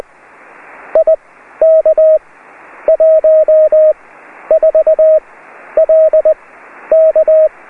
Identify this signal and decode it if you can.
Morse code, heard as a keyed tone, about 20 wpm, spaced as for 11 wpm, 630 Hz. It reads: IK14LK